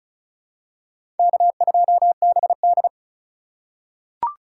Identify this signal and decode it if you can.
Morse code, heard as a keyed tone, about 36 wpm, 700 Hz. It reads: K2BD
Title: USA Call Signs - 4 Characters - 4x Character Spacing - 35wpm